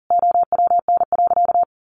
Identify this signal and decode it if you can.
Morse code, heard as a keyed tone, about 40 wpm, 700 Hz. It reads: OWN.
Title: Sentences from Top 1400 Words 40wpm